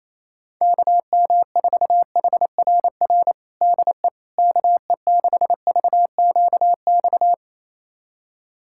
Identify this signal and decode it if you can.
Morse code, heard as a keyed tone, about 28 wpm, 700 Hz. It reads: KM4HRR DE KE6VQX
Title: USA Callsign DE Callsign 28wpm